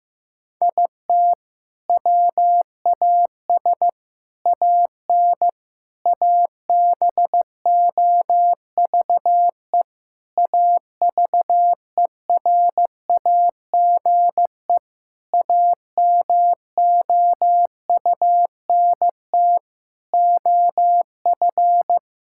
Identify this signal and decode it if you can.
Morse code, heard as a keyed tone, about 15 wpm, 700 Hz. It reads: IT WAS AN ABOVE AVERAGE AMOUNT OF
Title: Sentences from Top 1000 Words - Review Part 6 - 15wpm